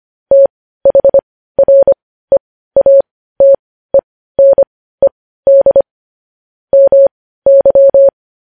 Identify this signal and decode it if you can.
Morse code, heard as a keyed tone, about 25 wpm, spaced as for 15 wpm, 550 Hz. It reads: THREATENED MY
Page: news headlines in morse code at 15 wpm